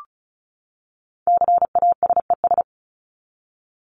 Morse code, heard as a keyed tone, about 34 wpm, 700 Hz. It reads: CASES
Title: ES:S Suffix - Top 100 Words with Base Words 35wpm